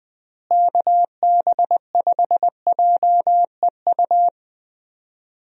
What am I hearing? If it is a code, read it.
Morse code, heard as a keyed tone, about 20 wpm, 700 Hz. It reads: KB5JEU
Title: USA Call Signs - 6 Characters - Repeated 2x - 20wpm